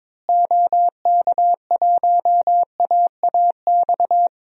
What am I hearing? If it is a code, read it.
Morse code, heard as a keyed tone, about 22 wpm, 700 Hz. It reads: OK1AAX
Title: DXCC-100 Call Signs - 6 Characters - Repeated 2x - 22wpm